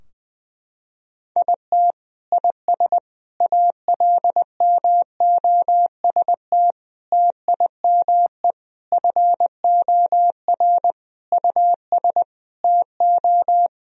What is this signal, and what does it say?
Morse code, heard as a keyed tone, about 21 wpm, 700 Hz. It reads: IT IS ALMOST TIME FOR US TO
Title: Sentences from Top 1000 Words - Review Part 4 - 20wpm